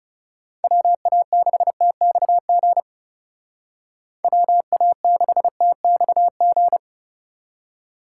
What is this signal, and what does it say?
Morse code, heard as a keyed tone, about 37 wpm, 700 Hz. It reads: WA6TXG WA6TXG
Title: Speed Racing - USA 2x3 Call Signs 30wpm